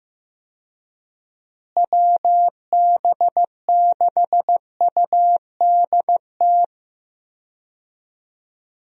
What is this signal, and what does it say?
Morse code, heard as a keyed tone, about 15 wpm, 700 Hz. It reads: WB6UDT
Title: USA Call Signs - 6 Characters - Repeated 3x - 15wpm